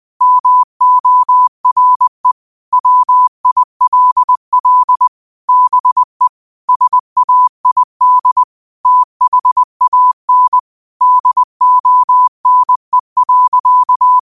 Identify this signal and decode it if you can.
Morse code, heard as a keyed tone, about 20 wpm, 1000 Hz. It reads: MORE WILL BE SAID THAN DONE.